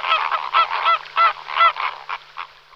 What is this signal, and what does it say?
There is no code.